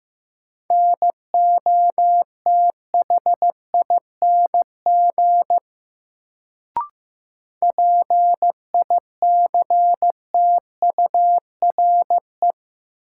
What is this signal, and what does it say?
Morse code, heard as a keyed tone, about 15 wpm, 700 Hz. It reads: NOTHING PICTURE
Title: Sentences from Top 500 Words - New Word Drill - 15wpm